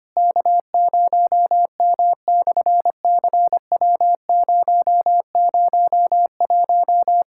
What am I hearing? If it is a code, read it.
Morse code, heard as a keyed tone, about 25 wpm, 700 Hz. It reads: K0M/CW001
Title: SOTA QSO Practice 25wpm